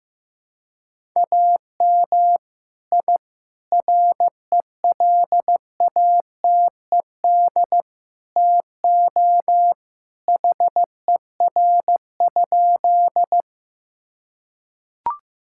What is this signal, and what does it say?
Morse code, heard as a keyed tone, about 15 wpm, 700 Hz. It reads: AM I RELATED TO HER?
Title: Sentences from Top 1000 Words - Review Part 5 - 15wpm